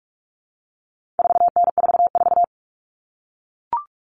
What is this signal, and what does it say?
Morse code, heard as a keyed tone, about 44 wpm, 700 Hz. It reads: 4N44
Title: All Cut Numbers - 4-Digit 45wpm